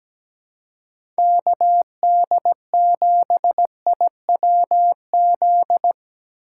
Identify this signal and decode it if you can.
Morse code, heard as a keyed tone, about 17 wpm, 700 Hz. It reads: KD7IWZ
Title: USA Call Signs - 6 Characters - 4x Character Spacing - 17wpm